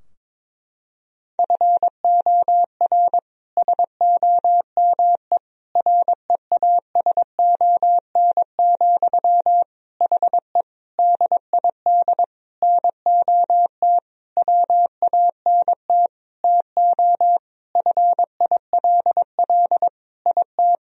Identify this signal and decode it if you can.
Morse code, heard as a keyed tone, about 22 wpm, 700 Hz. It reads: FOR SOME REASON, HE DID NOT WANT TO FILL IT
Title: Sentences from Top 1000 Words - Review Part 3 - 22wpm